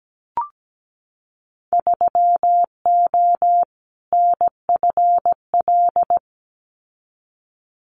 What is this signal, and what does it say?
Morse code, heard as a keyed tone, about 17 wpm, 700 Hz. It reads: 3O NFL
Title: Winter Field Day Exchanges 17wpm